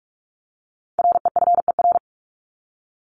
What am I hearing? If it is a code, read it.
Morse code, heard as a keyed tone, about 44 wpm, 700 Hz. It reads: REFER
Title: Top 751-1000 Words 45wpm